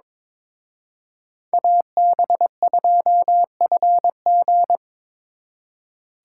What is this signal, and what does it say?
Morse code, heard as a keyed tone, about 27 wpm, 700 Hz. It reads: AB2FG
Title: Speed Racing - USA 2x2 Call Signs 22wpm